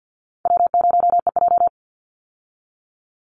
Morse code, heard as a keyed tone, about 50 wpm, 700 Hz. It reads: W0EJ
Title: Easy Call Signs - 50wpm